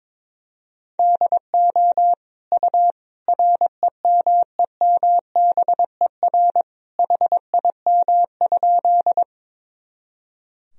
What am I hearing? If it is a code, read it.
Morse code, heard as a keyed tone, about 22 wpm, 700 Hz. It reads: DO U REMEMBER HIM?
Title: Sentences from Top 400 Words and Abbreviations 22wpm